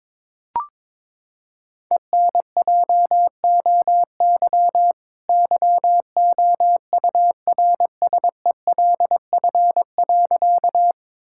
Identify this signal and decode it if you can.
Morse code, heard as a keyed tone, about 22 wpm, 700 Hz. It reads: ENJOY YOURSELF.
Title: Sentences from Top 1000 Words - Review Part 5 - 22wpm